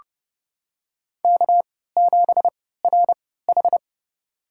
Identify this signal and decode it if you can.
Morse code, heard as a keyed tone, about 30 wpm, spaced as for 17 wpm, 700 Hz. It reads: K7RH
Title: USA Call Signs - 4 Characters - 3x Character Spacing - 30wpm